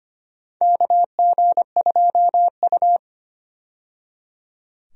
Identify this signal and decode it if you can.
Morse code, heard as a keyed tone, about 25 wpm, 700 Hz. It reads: KG2U
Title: USA Call Signs - 4 Characters - 25wpm